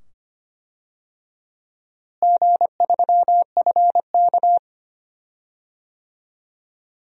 Morse code, heard as a keyed tone, about 25 wpm, 700 Hz. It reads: G3FK
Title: DXCC-100 Call Signs - 4 Characters - Repeated 2x - 25wpm